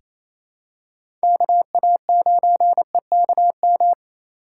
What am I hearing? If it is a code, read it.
Morse code, heard as a keyed tone, about 28 wpm, 700 Hz. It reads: KA9EKM
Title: USA Call Signs - 2x Character Spacing - 28wpm